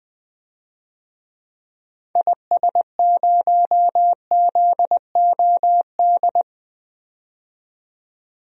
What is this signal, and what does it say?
Morse code, heard as a keyed tone, about 20 wpm, 700 Hz. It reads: IS0ZOD
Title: DXCC-100 Call Signs - 6 Characters - 20wpm